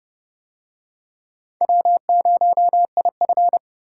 Morse code, heard as a keyed tone, about 30 wpm, 700 Hz. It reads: W0IF